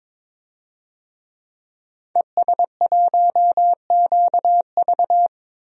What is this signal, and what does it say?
Morse code, heard as a keyed tone, about 22 wpm, 700 Hz. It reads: ES1QV
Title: DXCC-100 Call Signs - 5 Characters - Repeated 3x - 22wpm